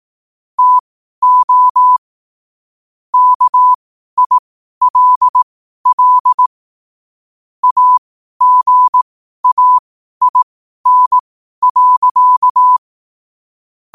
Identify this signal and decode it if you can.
Morse code, heard as a keyed tone, about 18 wpm, spaced as for 12 wpm, 1000 Hz. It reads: TO KILL AGAIN.
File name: qotdfiles / 12wpm